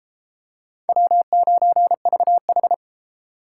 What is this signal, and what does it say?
Morse code, heard as a keyed tone, about 30 wpm, 700 Hz. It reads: W9VH